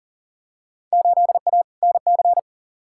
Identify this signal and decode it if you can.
Morse code, heard as a keyed tone, about 40 wpm, 700 Hz. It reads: 8A NC
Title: Field Day Exchanges - 40wpm